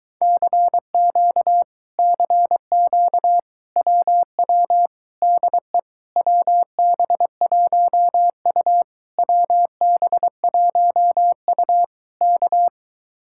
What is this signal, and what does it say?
Morse code, heard as a keyed tone, about 22 wpm, 700 Hz. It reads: CQ CQ WW DE WB1U WB1U K